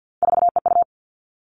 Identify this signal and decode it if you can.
Morse code, heard as a keyed tone, about 50 wpm, 700 Hz. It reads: 4EU
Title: Three Challenging Letter-Number Combinations 50wpm